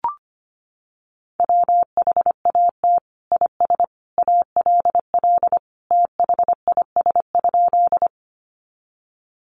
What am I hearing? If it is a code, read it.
Morse code, heard as a keyed tone, about 25 wpm, 700 Hz. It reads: WHAT IS ALL THIS?
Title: Sentences from Top 100 Words 25wpm